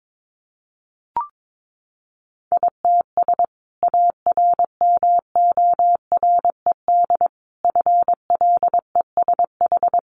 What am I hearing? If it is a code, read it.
Morse code, heard as a keyed tone, about 22 wpm, 700 Hz. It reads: ITS ARMORED FLESH
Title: A Sound Of Thunder 22wpm